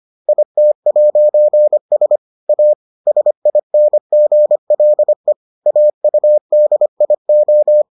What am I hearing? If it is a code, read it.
Morse code, heard as a keyed tone, about 25 wpm, 600 Hz. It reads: IT'S A SINGLE AUDIO